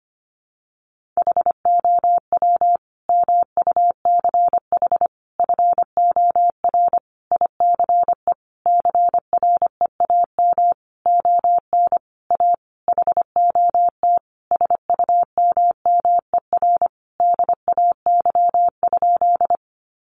Morse code, heard as a keyed tone, about 25 wpm, 700 Hz. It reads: HOW MUCH FOR ICE CREAM ON A HOT SUMMER DAY?
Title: Sentences from Top 1500 Words 25wpm